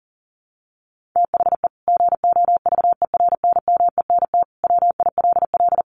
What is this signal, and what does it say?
Morse code, heard as a keyed tone, about 40 wpm, 700 Hz. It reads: THE GOVERNMENT WILL
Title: Sentences from Top 1500 Words 40wpm